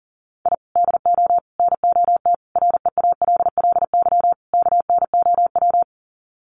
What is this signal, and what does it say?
Morse code, heard as a keyed tone, about 40 wpm, 700 Hz. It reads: I DO NOT REALLY KNOW
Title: Sets of 5 Words Encore 40wpm